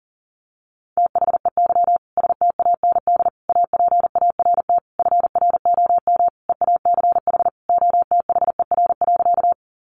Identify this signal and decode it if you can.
Morse code, heard as a keyed tone, about 40 wpm, 700 Hz. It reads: THEY STAND APART FROM EACH OTHER.
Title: Easy Phrases 40wpm